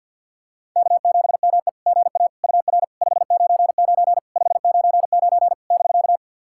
Morse code, heard as a keyed tone, about 50 wpm, 700 Hz. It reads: K7ME GA UR 599 599 <BK>
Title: SOTA QSO Practice 50wpm